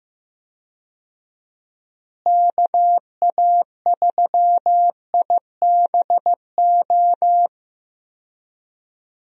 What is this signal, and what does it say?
Morse code, heard as a keyed tone, about 15 wpm, 700 Hz. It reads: KA3IBO
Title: USA Call Signs - 6 Characters - Repeated 2x - 15wpm